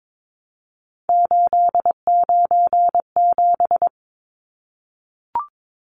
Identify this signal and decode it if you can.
Morse code, heard as a keyed tone, about 22 wpm, 700 Hz. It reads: 897
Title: Numbers - 3-Digit 22wpm